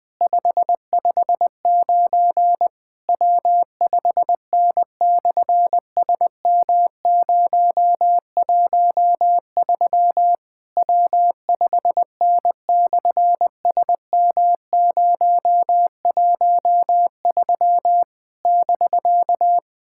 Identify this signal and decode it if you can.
Morse code, heard as a keyed tone, about 19 wpm, 700 Hz. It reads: 559 W5N/SM013 W5N/SM013 <BK>
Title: SOTA QSO Practice 20wpm